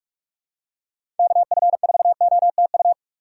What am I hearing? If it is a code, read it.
Morse code, heard as a keyed tone, about 46 wpm, 700 Hz. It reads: KF4OTU
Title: USA Call Signs - 6 Characters - 6x Character Spacing - 45wpm